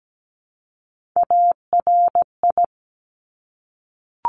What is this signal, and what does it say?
Morse code, heard as a keyed tone, about 17 wpm, 700 Hz. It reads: ARI
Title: Top Three Letter Combinations in Words 17wpm